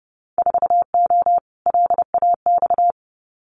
Morse code, heard as a keyed tone, about 30 wpm, 700 Hz. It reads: 4O LAX
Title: Winter Field Day Exchanges Encore 30wpm